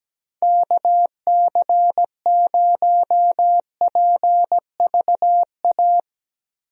Morse code, heard as a keyed tone, about 17 wpm, 700 Hz. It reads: KC0PVA
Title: USA Call Signs - 6 Characters - Repeated 2x - 17wpm